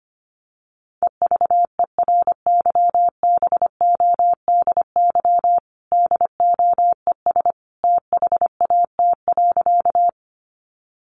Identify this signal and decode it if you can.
Morse code, heard as a keyed tone, about 25 wpm, 700 Hz. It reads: EVERYBODY DOES THAT.
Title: Sentences from Top 1400 Words 25wpm